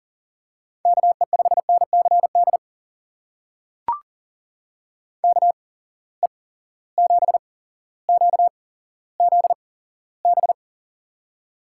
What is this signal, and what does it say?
Morse code, heard as a keyed tone, about 39 wpm, spaced as for 17 wpm, 700 Hz. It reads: KE5NCD KE7QZB
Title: USA Call Signs - 6 Characters - 8x Character Spacing - 40wpm